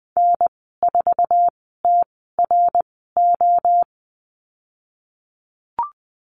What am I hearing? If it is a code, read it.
Morse code, heard as a keyed tone, about 20 wpm, spaced as for 15 wpm, 700 Hz. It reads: N4TRO
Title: 055 Lesson 10g - Letter r - Call Signs 20wpm